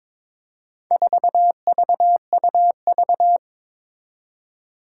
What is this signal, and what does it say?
Morse code, heard as a keyed tone, about 22 wpm, 700 Hz. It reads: 4VUV
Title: Four Challenging Letter-Number Combinations 22wpm